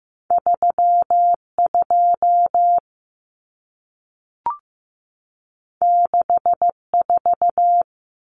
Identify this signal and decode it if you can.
Morse code, heard as a keyed tone, about 15 wpm, 700 Hz. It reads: 32 64
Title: Common Cut Numbers - 2-Digit 15wpm